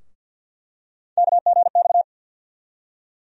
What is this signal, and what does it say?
Morse code, heard as a keyed tone, about 51 wpm, 700 Hz. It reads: KGX
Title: USA Call Sign Postfixes - 3 Letters 50wpm